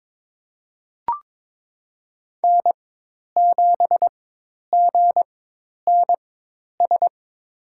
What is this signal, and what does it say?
Morse code, heard as a keyed tone, about 21 wpm, spaced as for 10 wpm, 700 Hz. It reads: N7GNS
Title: USA Call Signs - 4x Character Spacing - 22wpm